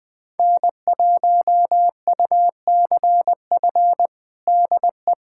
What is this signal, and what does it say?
Morse code, heard as a keyed tone, about 20 wpm, 700 Hz. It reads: N1UCF DE
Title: USA Callsign DE Callsign - 5 Characters - 20wpm